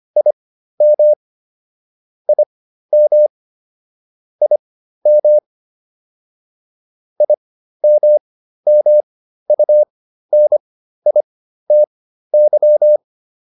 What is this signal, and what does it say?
Morse code, heard as a keyed tone, about 25 wpm, spaced as for 13 wpm, 600 Hz. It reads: IM IM IM IMMUNITY